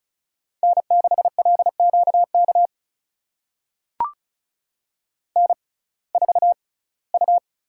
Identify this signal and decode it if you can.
Morse code, heard as a keyed tone, about 36 wpm, spaced as for 21 wpm, 700 Hz. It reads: N6LQK N4U
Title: USA Call Signs - 5 Characters - 6x Character Spacing - 35wpm